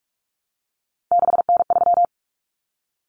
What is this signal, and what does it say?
Morse code, heard as a keyed tone, about 44 wpm, 700 Hz. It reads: 6N3
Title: Common Cut Numbers - 3-Digit 45wpm